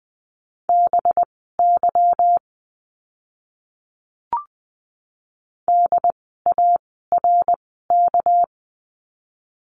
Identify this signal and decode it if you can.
Morse code, heard as a keyed tone, about 20 wpm, spaced as for 15 wpm, 700 Hz. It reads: BY DARK
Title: 199 Lesson 32f - Letter k - Words 20wpm